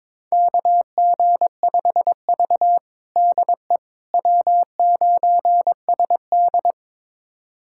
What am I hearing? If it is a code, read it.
Morse code, heard as a keyed tone, about 22 wpm, 700 Hz. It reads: KG5V DE W9SD
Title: USA Callsign DE Callsign - 4 Characters - 22wpm